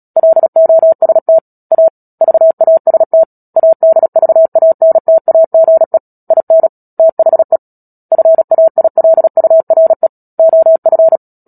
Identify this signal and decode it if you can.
Morse code, heard as a keyed tone, about 37 wpm, 650 Hz. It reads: LOST A VAST ADVANTAGE IN THE FAILURE OF
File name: Book Two / 10-Chapter 10 - Epilogue - 36 WPM